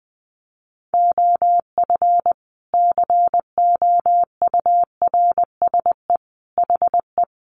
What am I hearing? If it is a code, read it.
Morse code, heard as a keyed tone, about 20 wpm, 700 Hz. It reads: OF COURSE HE
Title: Sentences from Top 1200 Words 20wpm